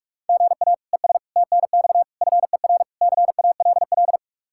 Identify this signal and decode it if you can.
Morse code, heard as a keyed tone, about 48 wpm, 700 Hz. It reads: GA ES TNX FER CALL